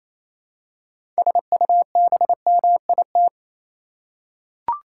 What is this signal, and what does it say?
Morse code, heard as a keyed tone, about 28 wpm, 700 Hz. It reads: SUBMIT
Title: Six Letter Words - 28wpm